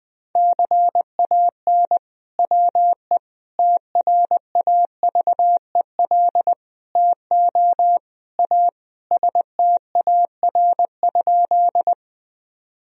Morse code, heard as a keyed tone, about 20 wpm, 700 Hz. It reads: CAN WE TRAVEL TO A STAR?
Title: Sentences from Top 1000 Words - Review Part 2 - 20wpm